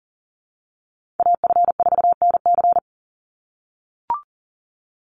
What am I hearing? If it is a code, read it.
Morse code, heard as a keyed tone, about 40 wpm, 700 Hz. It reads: AF4NC
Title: Call Signs - 40wpm